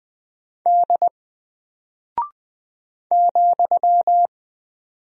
Morse code, heard as a keyed tone, about 20 wpm, 700 Hz. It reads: D ,
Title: Single Character (Letter:Number:Punctuation:Prosign) Alt 20wpm